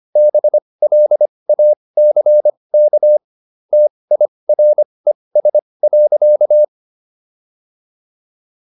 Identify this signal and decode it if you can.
Morse code, heard as a keyed tone, about 25 wpm, spaced as for 20 wpm, 600 Hz. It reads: BLACK TIRES.